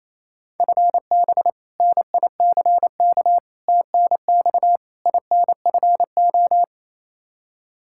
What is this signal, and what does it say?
Morse code, heard as a keyed tone, about 27 wpm, 700 Hz. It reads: FB NICK TNX INFO
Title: Practice QSO - Exchange 3 - More Info - 28wpm